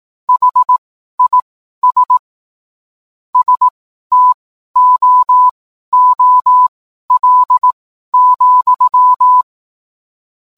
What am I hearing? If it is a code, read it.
Morse code, heard as a keyed tone, about 18 wpm, spaced as for 12 wpm, 1000 Hz. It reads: HIS STOOL,